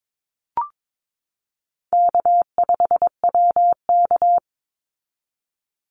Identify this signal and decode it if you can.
Morse code, heard as a keyed tone, about 22 wpm, 700 Hz. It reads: K5WK
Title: Easy Call Signs Repeated 2x - 22wpm